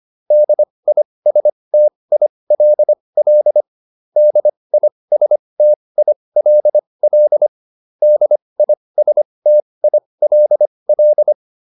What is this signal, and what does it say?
Morse code, heard as a keyed tone, about 25 wpm, spaced as for 20 wpm, 600 Hz. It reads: DISTILL DISTILL DISTILL